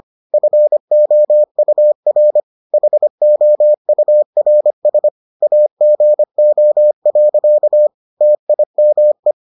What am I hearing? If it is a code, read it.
Morse code, heard as a keyed tone, about 25 wpm, 600 Hz. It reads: FOUR HOURS AGO. TIME